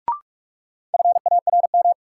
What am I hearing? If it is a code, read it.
Morse code, heard as a keyed tone, about 46 wpm, 700 Hz. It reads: FARM
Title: Sentences from Top 1200 Words - New Word Drill Alt - 45wpm